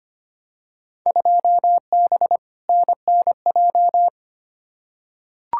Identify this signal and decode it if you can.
Morse code, heard as a keyed tone, about 25 wpm, 700 Hz. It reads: 2B NNJ
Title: Field Day Exchanges Encore - 25wpm